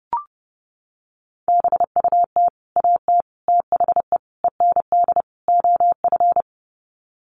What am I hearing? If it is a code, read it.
Morse code, heard as a keyed tone, about 30 wpm, 700 Hz. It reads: BUT AT THE END OF
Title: Sets of 5 Words Encore2 30wpm